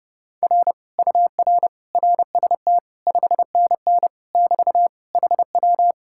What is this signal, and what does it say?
Morse code, heard as a keyed tone, about 30 wpm, 700 Hz. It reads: R UR RST 5NN <BT> HW